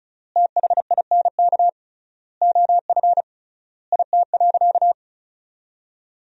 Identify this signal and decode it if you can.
Morse code, heard as a keyed tone, about 36 wpm, 700 Hz. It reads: THINK OF IT.
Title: 3x Word-Spacing - Sentences from Top 200 Words - 35wpm